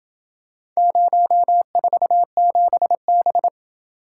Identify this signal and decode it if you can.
Morse code, heard as a keyed tone, about 27 wpm, 700 Hz. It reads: 047B